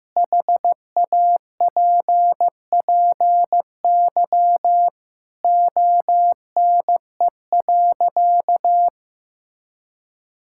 Morse code, heard as a keyed tone, about 15 wpm, 700 Hz. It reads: HAPPY ONE.
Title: Sentences from Top 1000 Words - Review Part 5 - 15wpm